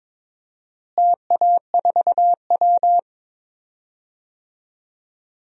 Morse code, heard as a keyed tone, about 22 wpm, 700 Hz. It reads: TA4W
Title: DXCC-100 Call Signs - 4 Characters - Repeated 3x - 22wpm